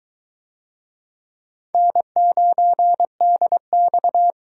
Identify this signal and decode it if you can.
Morse code, heard as a keyed tone, about 22 wpm, 700 Hz. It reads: N9DX